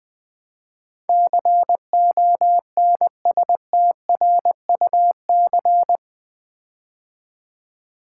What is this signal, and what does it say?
Morse code, heard as a keyed tone, about 21 wpm, 700 Hz. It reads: CONSTRUC
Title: Build A Word - 4 Letters At A Time - 12-Letter Words - 20wpm